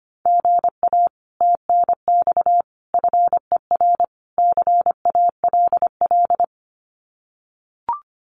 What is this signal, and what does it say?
Morse code, heard as a keyed tone, about 25 wpm, 700 Hz. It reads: GA TNX FER CALL
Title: Common QSO Phrases Encore 25wpm